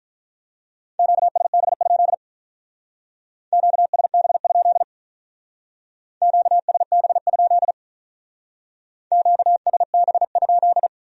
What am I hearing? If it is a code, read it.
Morse code, heard as a keyed tone, about 43 wpm, 700 Hz. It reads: QSB? QSB? QSB? QSB?
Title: Speed-Racing Q-Codes - 35wpm